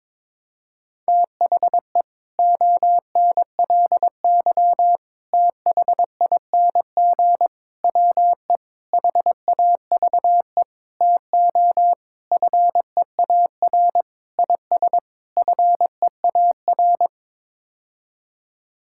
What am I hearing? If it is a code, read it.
Morse code, heard as a keyed tone, about 22 wpm, 700 Hz. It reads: THE ONLY THING WE HAVE TO FEAR IS FEAR
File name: Sets of 9 Words - 22wpm